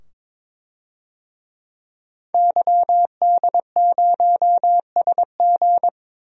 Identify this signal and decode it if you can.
Morse code, heard as a keyed tone, about 22 wpm, 700 Hz. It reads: YD0SG
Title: DXCC-100 Call Signs - 5 Characters - 22wpm